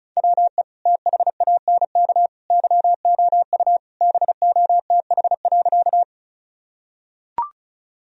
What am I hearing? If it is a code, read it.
Morse code, heard as a keyed tone, about 34 wpm, 700 Hz. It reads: WE THANK YOU BOTH.
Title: Sentences from Top 1000 Words - Review Part 1 - 35wpm